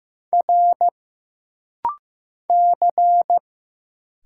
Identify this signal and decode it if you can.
Morse code, heard as a keyed tone, about 15 wpm, 700 Hz. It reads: R C
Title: Single Letter-Number Alt - Rapid-Fire 15wpm